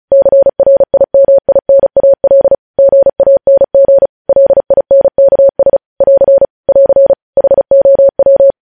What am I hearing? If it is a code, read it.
Morse code, heard as a keyed tone, about 36 wpm, 550 Hz. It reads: CRIMINAL GANG LINKS <AR> <AR> HOW
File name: podcast / News headlines-1x-35x35